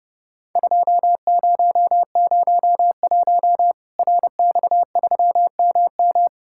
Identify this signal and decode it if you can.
Morse code, heard as a keyed tone, about 30 wpm, 700 Hz. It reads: 2001 RX3MM
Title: Worked All Europe DX Contest - QTC Exchange - 30wpm